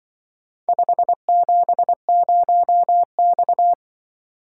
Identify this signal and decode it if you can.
Morse code, heard as a keyed tone, about 24 wpm, 700 Hz. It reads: 570X
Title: Speed-Racing - Four Letter-Number Combinations - 20wpm